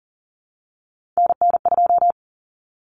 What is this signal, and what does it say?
Morse code, heard as a keyed tone, about 40 wpm, 700 Hz. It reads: NN2
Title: Common Cut Numbers - 3-Digit 40wpm